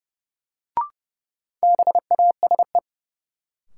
Morse code, heard as a keyed tone, about 30 wpm, 700 Hz. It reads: BASE
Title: Sentences from Top 1600 Words - New Word Drill Alt - 30wpm